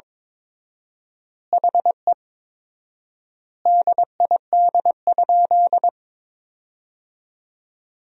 Morse code, heard as a keyed tone, about 22 wpm, 700 Hz. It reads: HE DID?
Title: 4x Word-Spacing - Sentences from Top 100 Words - 22wpm